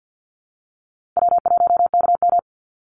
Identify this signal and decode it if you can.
Morse code, heard as a keyed tone, about 48 wpm, 700 Hz. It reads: W1KM
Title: Easy Call Signs Repeated 3x - 50wpm